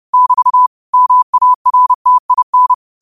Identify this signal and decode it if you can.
Morse code, heard as a keyed tone, about 30 wpm, 1000 Hz. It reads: <BT> MARTIN
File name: qotdfiles / 30wpm